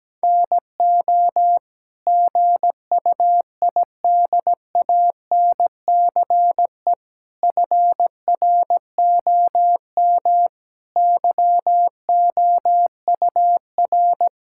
Morse code, heard as a keyed tone, about 17 wpm, 700 Hz. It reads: NO GUIDANCE FROM YOUR
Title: Sentences from Top 1600 Words 17wpm